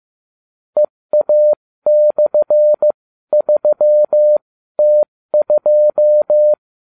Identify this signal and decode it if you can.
Morse code, heard as a keyed tone, about 15 wpm, spaced as for 13 wpm, 600 Hz. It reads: EA/3T2